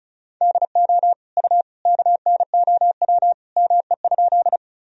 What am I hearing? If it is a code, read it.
Morse code, heard as a keyed tone, about 36 wpm, 700 Hz. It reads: DO U KNOW ME?
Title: Sentences from Top 200 Words and Abbreviations 35wpm